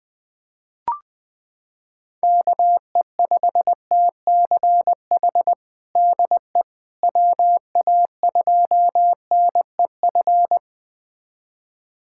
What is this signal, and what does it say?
Morse code, heard as a keyed tone, about 20 wpm, 700 Hz. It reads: KE5TCH DE WA2NEF
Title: USA Callsign DE Callsign 20wpm